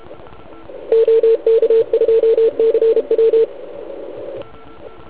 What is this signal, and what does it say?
Morse code, heard as a keyed tone, about 32 wpm, 450 Hz. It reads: OK2CW
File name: Archiv / CUC149 / OK1IF CUC149